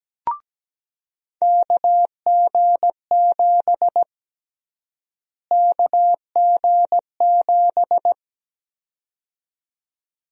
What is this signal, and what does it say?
Morse code, heard as a keyed tone, about 17 wpm, 700 Hz. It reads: KG7 KG7
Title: USA Call Sign Prefixes - Repeated 2x - 17wpm